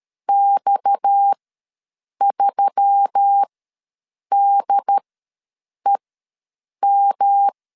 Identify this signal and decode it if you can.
Morse code, heard as a keyed tone, about 13 wpm, spaced as for 7 wpm, 800 Hz. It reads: X3DEM
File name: LEZIONI / 7 GF9Z / 7 G ws 12wpm 3140